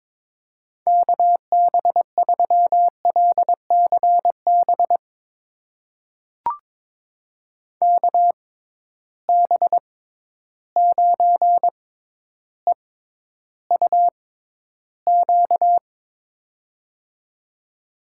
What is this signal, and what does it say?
Morse code, heard as a keyed tone, about 22 wpm, spaced as for 11 wpm, 700 Hz. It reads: KB3LCB KB9EUQ
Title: USA Call Signs - 6 Characters - 6x Character Spacing - 22wpm